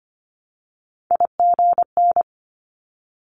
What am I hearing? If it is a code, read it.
Morse code, heard as a keyed tone, about 25 wpm, 700 Hz. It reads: IGN
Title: Multi-letter Phonemes 25wpm